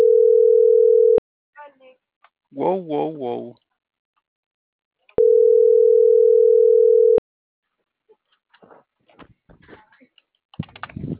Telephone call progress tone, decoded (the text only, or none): ringback tone